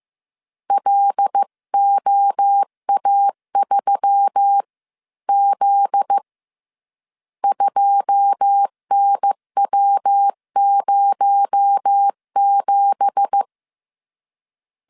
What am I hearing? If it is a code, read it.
Morse code, heard as a keyed tone, about 15 wpm, 800 Hz. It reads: LOA3Z 2NW07
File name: CW / DET / 1 9 1500C S C 14wpm 3245